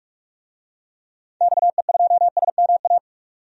Morse code, heard as a keyed tone, about 46 wpm, 700 Hz. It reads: XE2SMA
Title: DXCC-100 Call Signs - 6 Characters - 45wpm